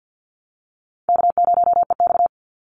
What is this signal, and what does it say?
Morse code, heard as a keyed tone, about 50 wpm, 700 Hz. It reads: K0EX
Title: Call Signs - 50wpm